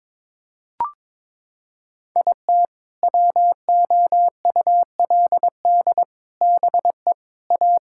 Morse code, heard as a keyed tone, about 22 wpm, 700 Hz. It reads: IT WOULD BE A
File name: Sentences with 7-Word Phrases - 22wpm